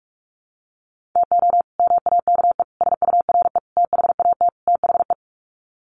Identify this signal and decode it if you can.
Morse code, heard as a keyed tone, about 44 wpm, 700 Hz. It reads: TO MAKE SURE THAT THE
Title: Sets of 5 Words 45wpm